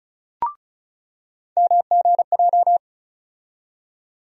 Morse code, heard as a keyed tone, about 36 wpm, 700 Hz. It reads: MGJ
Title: USA Call Sign Postfixes - 3 Letters 35wpm